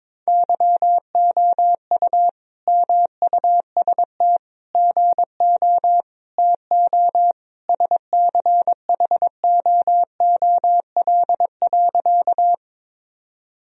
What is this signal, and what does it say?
Morse code, heard as a keyed tone, about 22 wpm, 700 Hz. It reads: YOU MUST GO TO SCHOOL.
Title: Sentences from Top 1000 Words - Review Part 1 - 22wpm